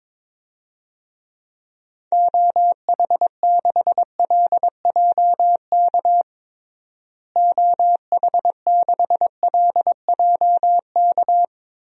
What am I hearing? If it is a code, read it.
Morse code, heard as a keyed tone, about 22 wpm, 700 Hz. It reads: OH6LJK OH6LJK
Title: DXCC-100 Call Signs - 6 Characters - Repeated 2x - 22wpm